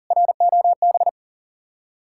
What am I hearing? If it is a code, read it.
Morse code, heard as a keyed tone, about 52 wpm, 700 Hz. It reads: ROB